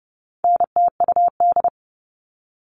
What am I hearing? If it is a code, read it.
Morse code, heard as a keyed tone, about 30 wpm, 700 Hz. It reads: NTUD
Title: All Cut Numbers - 4-Digit 30wpm